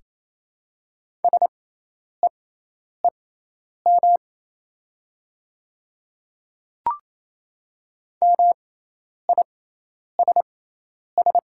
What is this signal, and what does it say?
Morse code, heard as a keyed tone, about 28 wpm, spaced as for 10 wpm, 700 Hz. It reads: SEEM MISS
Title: 6x Character Spacing - Top 500 Words - 28wpm